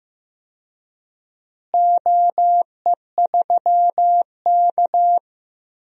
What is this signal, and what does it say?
Morse code, heard as a keyed tone, about 15 wpm, 700 Hz. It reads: OE3K
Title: DXCC-100 Call Signs - 4 Characters - Repeated 3x - 15wpm